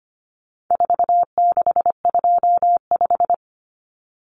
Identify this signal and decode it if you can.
Morse code, heard as a keyed tone, about 25 wpm, 700 Hz. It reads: 4625
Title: Common Cut Numbers - 4-Digit 25wpm